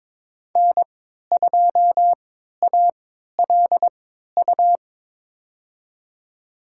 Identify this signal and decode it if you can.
Morse code, heard as a keyed tone, about 22 wpm, spaced as for 12 wpm, 700 Hz. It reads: N2ALU
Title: USA Call Signs - 3x Character Spacing - 22wpm